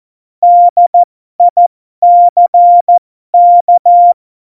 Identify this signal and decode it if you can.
Morse code, heard as a keyed tone, about 14 wpm, spaced as for 12 wpm, 700 Hz. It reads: DICK
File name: Morsekurs / Witze